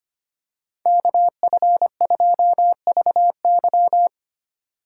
Morse code, heard as a keyed tone, about 25 wpm, 700 Hz. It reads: KF2VY
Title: USA Call Signs - 5 Characters - 2x Character Spacing - 25wpm